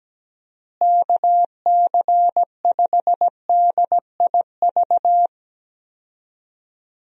Morse code, heard as a keyed tone, about 17 wpm, 700 Hz. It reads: KC5DIV
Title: USA Call Signs - 6 Characters - Repeated 2x - 17wpm